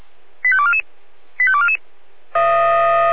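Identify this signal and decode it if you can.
Five-tone selective call (ZVEI), heard as quick runs of five tones, tones of about 70 ms. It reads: 85244 85244
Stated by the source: ZVEI five-tone selective call received on an SDR.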